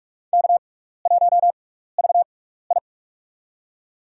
Morse code, heard as a keyed tone, about 44 wpm, spaced as for 15 wpm, 700 Hz. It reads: K1VI
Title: USA Call Signs - 4 Characters - 6x Character Spacing - 45wpm